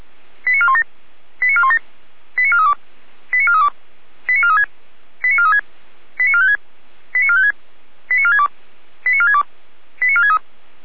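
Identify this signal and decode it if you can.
Five-tone selective call (ZVEI), heard as quick runs of five tones, tones of about 70 ms. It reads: 89417 89417 89432 89432 89436 89436 89456 89456 89462 89462 89463
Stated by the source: ZVEI five-tone selective call received on an SDR.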